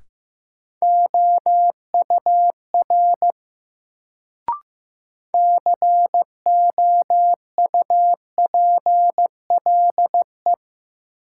Morse code, heard as a keyed tone, about 15 wpm, 700 Hz. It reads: OUR COUPLE
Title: Sets of 2 Words - New Word Drill - 15wpm